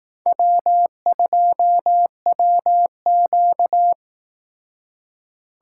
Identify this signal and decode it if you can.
Morse code, heard as a keyed tone, about 19 wpm, 700 Hz. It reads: W2WQ